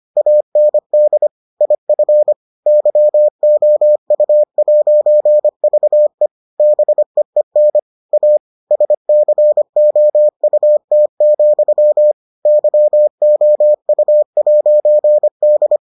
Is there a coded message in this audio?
Yes: Morse code from 0.2 to 15.8 s, heard as a keyed tone, about 25 wpm, 600 Hz. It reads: AND IF YOU'VE BEEN A SCOUT, YOU'D